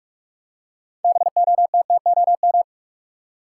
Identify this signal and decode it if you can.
Morse code, heard as a keyed tone, about 44 wpm, 700 Hz. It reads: BOTTOM
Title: Six Letter Words - 45wpm